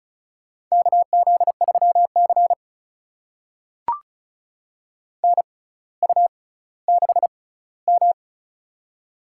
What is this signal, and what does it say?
Morse code, heard as a keyed tone, about 36 wpm, spaced as for 18 wpm, 700 Hz. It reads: KZ3C NU6M
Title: USA Call Signs - 4 Characters - 6x Character Spacing - 35wpm